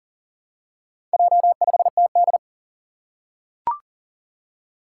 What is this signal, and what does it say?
Morse code, heard as a keyed tone, about 41 wpm, 700 Hz. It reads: J5TD
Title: Four Challenging Letter-Number Combinations 40wpm